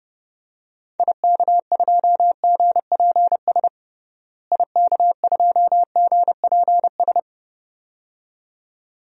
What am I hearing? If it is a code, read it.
Morse code, heard as a keyed tone, about 30 wpm, 700 Hz. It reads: IK2GPS IK2GPS
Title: DXCC-100 Call Signs - 6 Characters - Repeated 2x - 30wpm